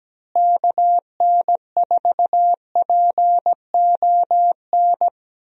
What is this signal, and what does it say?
Morse code, heard as a keyed tone, about 17 wpm, 700 Hz. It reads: KN4PON
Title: USA Call Signs - 6 Characters - Repeated 3x - 17wpm